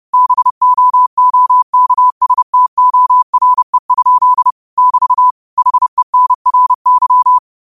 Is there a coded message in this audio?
Yes: Morse code from 0.1 to 7.4 s, heard as a keyed tone, about 30 wpm, 1000 Hz. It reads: BOOKSTORE? <BT> HENRY